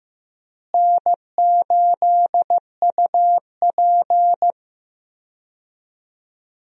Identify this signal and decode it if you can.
Morse code, heard as a keyed tone, about 15 wpm, 700 Hz. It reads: N8UP
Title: USA Call Signs - 4 Characters - Repeated 3x - 15wpm